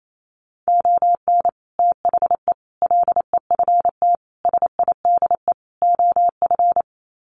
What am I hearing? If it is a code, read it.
Morse code, heard as a keyed tone, about 28 wpm, 700 Hz. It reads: ON THE LEFT SIDE OF
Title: Sets of 5 Words Encore2 28wpm